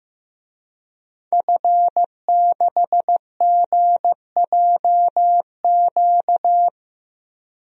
Morse code, heard as a keyed tone, about 15 wpm, 700 Hz. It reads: F6GJQ
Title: DXCC-100 Call Signs - 5 Characters - Repeated 2x - 15wpm